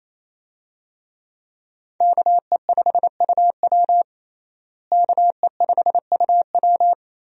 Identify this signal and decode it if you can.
Morse code, heard as a keyed tone, about 28 wpm, 700 Hz. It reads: KE5UW KE5UW
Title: USA Call Signs - 5 Characters - Repeated 2x - 28wpm